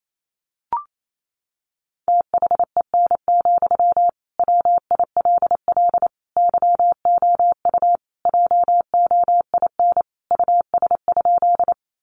Morse code, heard as a keyed tone, about 28 wpm, 700 Hz. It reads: THEN, WILL YOU JOIN US?
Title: Sentences from Top 1400 Words 28wpm